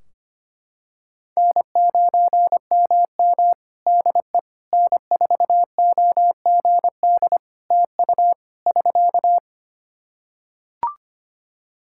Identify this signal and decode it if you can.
Morse code, heard as a keyed tone, about 25 wpm, 700 Hz. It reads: N9MM DE N4OGD TU <SK>
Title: K1USN SST QSO Practice 25wpm